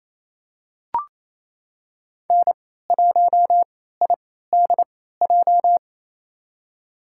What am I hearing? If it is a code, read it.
Morse code, heard as a keyed tone, about 28 wpm, spaced as for 16 wpm, 700 Hz. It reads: N1IDJ
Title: USA Call Signs - 3x Character Spacing - 28wpm